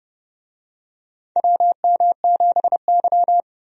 Morse code, heard as a keyed tone, about 29 wpm, 700 Hz. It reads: WM7Y
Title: USA Call Signs - 4 Characters - 30wpm